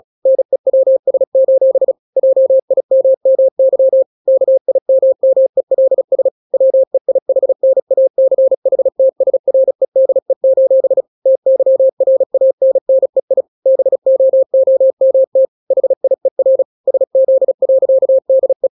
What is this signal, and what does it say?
Morse code, heard as a keyed tone, about 36 wpm, 530 Hz. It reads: NEWS: JIMMY KIMMELS WEIHNACHTSREDE: TYRANNEI BOOMT HIER SZ.DE
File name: daily / 35wpm